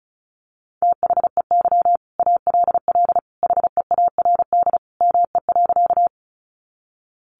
Morse code, heard as a keyed tone, about 34 wpm, 700 Hz. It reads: THEY ALL HEARD ME.
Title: Sentences from Top 400 Words 35wpm